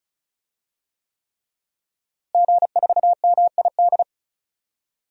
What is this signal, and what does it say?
Morse code, heard as a keyed tone, about 36 wpm, 700 Hz. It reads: G4MID
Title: DXCC-100 Call Signs - 5 Characters - Repeated 3x - 35wpm